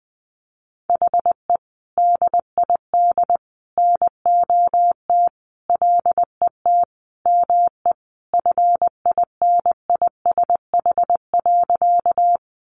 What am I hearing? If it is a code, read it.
Morse code, heard as a keyed tone, about 19 wpm, 700 Hz. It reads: HE DID NOT LET ME FINISH.
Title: Sentences from Top 1200 Words 20wpm